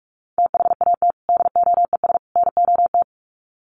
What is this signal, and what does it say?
Morse code, heard as a keyed tone, about 44 wpm, 700 Hz. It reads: THAT DOES NOT
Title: Sets of 3 Words 45wpm